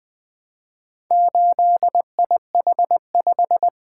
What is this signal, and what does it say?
Morse code, heard as a keyed tone, about 20 wpm, 700 Hz. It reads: 8IH5
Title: Four Letter-Number Combinations - 20wpm